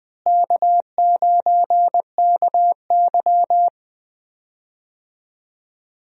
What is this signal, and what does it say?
Morse code, heard as a keyed tone, about 20 wpm, 700 Hz. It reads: K9KY
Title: USA Call Signs - 4 Characters - Repeated 2x - 20wpm